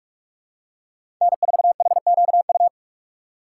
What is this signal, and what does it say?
Morse code, heard as a keyed tone, about 43 wpm, 700 Hz. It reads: N4HQU
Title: USA Call Signs - 2x Character Spacing - 45wpm